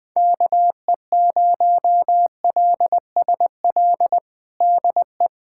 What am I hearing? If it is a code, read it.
Morse code, heard as a keyed tone, about 20 wpm, 700 Hz. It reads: KE0LSL DE
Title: USA Callsign DE Callsign 20wpm